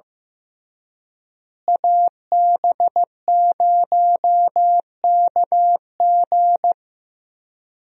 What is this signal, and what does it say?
Morse code, heard as a keyed tone, about 15 wpm, 700 Hz. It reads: AB0KG
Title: USA Call Signs - 5 Characters - Repeated 3x - 15wpm